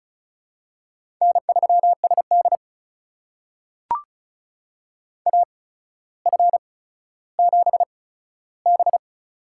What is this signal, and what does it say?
Morse code, heard as a keyed tone, about 36 wpm, spaced as for 15 wpm, 700 Hz. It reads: N3SD AF7B
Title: USA Call Signs - 4 Characters - 8x Character Spacing - 35wpm